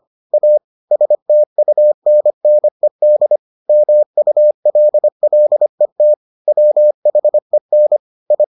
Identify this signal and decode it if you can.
Morse code, heard as a keyed tone, about 25 wpm, 600 Hz. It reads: A STUNNED MULLET WHEN I